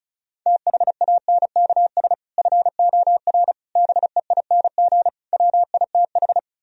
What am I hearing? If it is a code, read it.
Morse code, heard as a keyed tone, about 46 wpm, 700 Hz. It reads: THANKS FOR BEING WITH